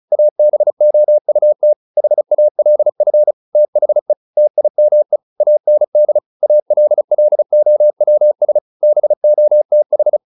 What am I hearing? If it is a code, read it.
Morse code, heard as a keyed tone, about 34 wpm, 600 Hz. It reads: ABOUT HALF THE TIME AND ALLOWS BOTH